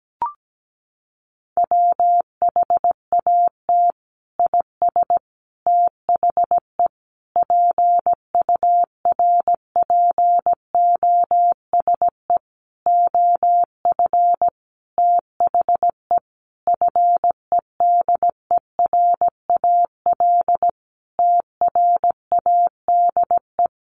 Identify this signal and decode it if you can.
Morse code, heard as a keyed tone, about 17 wpm, 700 Hz. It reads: WHAT IS THE PURPOSE OF THE FEDERAL TRADE
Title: Sentences from Top 1400 Words 17wpm